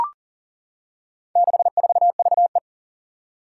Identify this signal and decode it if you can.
Morse code, heard as a keyed tone, about 37 wpm, 700 Hz. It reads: 64VE
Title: Four Challenging Letter-Number Combinations 40wpm